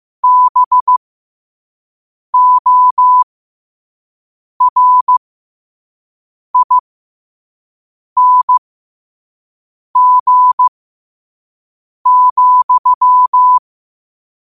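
Morse code, heard as a keyed tone, about 15 wpm, spaced as for 5 wpm, 1000 Hz. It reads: BORING,
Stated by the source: Morse quote-of-the-day practice at 5 wpm.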